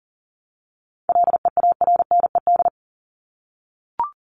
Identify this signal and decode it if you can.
Morse code, heard as a keyed tone, about 40 wpm, 700 Hz. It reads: LEARNED
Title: ED Suffix - Top 100 Words with Base Words 40wpm